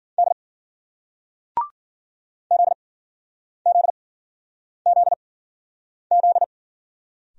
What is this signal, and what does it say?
Morse code, heard as a keyed tone, about 56 wpm, spaced as for 9 wpm, 700 Hz. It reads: D ZZZZ